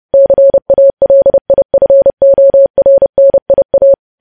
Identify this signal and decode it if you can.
Morse code, heard as a keyed tone, about 30 wpm, 550 Hz. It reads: CALIFORNIA
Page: news headlines in morse code at 30 wpm